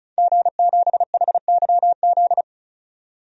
Morse code, heard as a keyed tone, about 34 wpm, 700 Hz. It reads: G7HYZ
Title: DXCC-100 Call Signs - 5 Characters - 35wpm